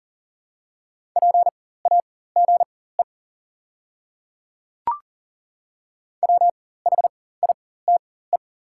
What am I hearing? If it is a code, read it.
Morse code, heard as a keyed tone, about 40 wpm, spaced as for 19 wpm, 690 Hz. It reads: PAGE WHITE
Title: 4x Character Spacing - Top 300 Words - 40wpm